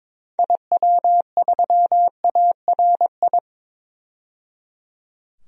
Morse code, heard as a keyed tone, about 22 wpm, 700 Hz. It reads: IW3ARI
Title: DXCC-100 Call Signs - 6 Characters - 22wpm